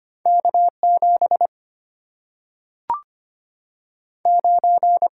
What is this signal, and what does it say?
Morse code, heard as a keyed tone, about 25 wpm, 700 Hz. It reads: K7 9
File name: Two Letter-Number Combinations - 25wpm